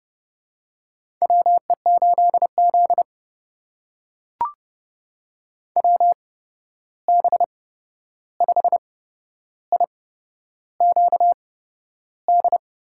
Morse code, heard as a keyed tone, about 31 wpm, spaced as for 11 wpm, 700 Hz. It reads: WE8Z WB5IQD
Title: USA Call Signs - 8x Character Spacing - 30wpm